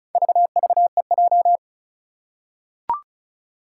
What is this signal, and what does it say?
Morse code, heard as a keyed tone, about 34 wpm, 700 Hz. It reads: VVEJ
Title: Four Challenging Letter-Number Combinations 35wpm